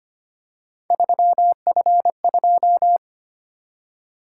Morse code, heard as a keyed tone, about 25 wpm, 700 Hz. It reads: 3F2
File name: Three Letter-Number Combinations - 25wpm